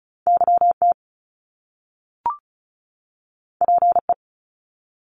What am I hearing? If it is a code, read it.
Morse code, heard as a keyed tone, about 36 wpm, 700 Hz. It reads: YT PE
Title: Canadian Provinces and Territories 35wpm